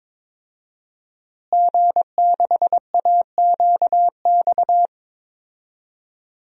Morse code, heard as a keyed tone, about 22 wpm, 700 Hz. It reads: G6AQX
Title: DXCC-100 Call Signs - 5 Characters - Repeated 3x - 22wpm